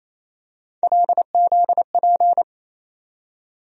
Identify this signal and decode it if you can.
Morse code, heard as a keyed tone, about 28 wpm, 700 Hz. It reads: LZP
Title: USA Call Sign Postfixes - 3 Letters 28wpm